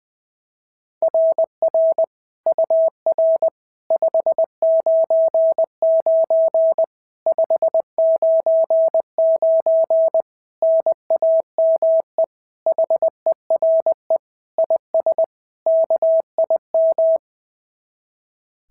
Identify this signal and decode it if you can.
Morse code, heard as a keyed tone, about 19 wpm, 650 Hz. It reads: RR UR 599 599 NAME HERE IS KIM
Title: SOTA QSO Practice 20wpm